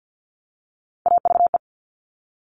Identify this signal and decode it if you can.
Morse code, heard as a keyed tone, about 50 wpm, 700 Hz. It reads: AVE
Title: Top Three Letter Combinations in Words 50wpm